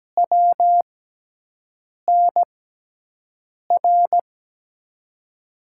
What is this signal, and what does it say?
Morse code, heard as a keyed tone, about 17 wpm, spaced as for 6 wpm, 700 Hz. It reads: WNR
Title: USA Call Signs - 6 Characters - 6x Character Spacing - 17wpm